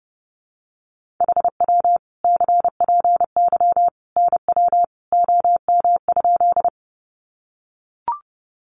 Morse code, heard as a keyed tone, about 28 wpm, 700 Hz. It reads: HW CPY NW OM?
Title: Common QSO Phrases 30wpm